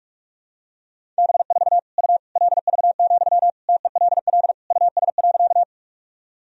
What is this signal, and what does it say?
Morse code, heard as a keyed tone, about 46 wpm, 700 Hz. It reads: B4 U LV, TELL US.
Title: Sentences from Top 400 Words and Abbreviations 45wpm